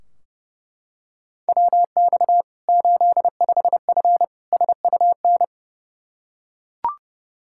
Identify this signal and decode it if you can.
Morse code, heard as a keyed tone, about 30 wpm, 700 Hz. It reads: WX 85F SUN
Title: Practice QSO - Exchange 2 - Info - 30wpm